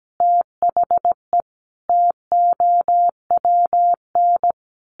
Morse code, heard as a keyed tone, about 17 wpm, 700 Hz. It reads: THE TOWN
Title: Sentences from Top 1300 Words 17wpm